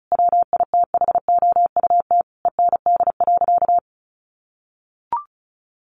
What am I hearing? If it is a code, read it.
Morse code, heard as a keyed tone, about 34 wpm, 700 Hz. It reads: WITHOUT END.
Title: Sentences from Top 600 Words 35wpm